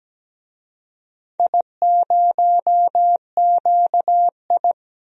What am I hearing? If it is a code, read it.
Morse code, heard as a keyed tone, about 17 wpm, 700 Hz. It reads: I0QI
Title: DXCC-100 Call Signs - 4 Characters - 17wpm